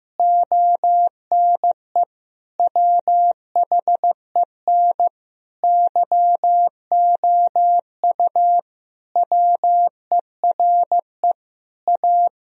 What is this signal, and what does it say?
Morse code, heard as a keyed tone, about 15 wpm, 700 Hz. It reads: ONE WHEN YOU WERE A
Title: Sentences from Top 1000 Words - Review Part 1 - 15wpm